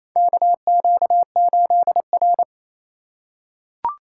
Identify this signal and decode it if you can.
Morse code, heard as a keyed tone, about 28 wpm, 700 Hz. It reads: KQ8R
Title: USA Call Signs - 4 Characters - 4x Character Spacing - 28wpm